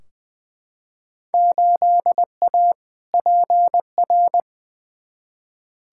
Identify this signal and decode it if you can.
Morse code, heard as a keyed tone, about 20 wpm, 700 Hz. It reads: 8A PR
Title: Field Day Exchanges Encore - 20wpm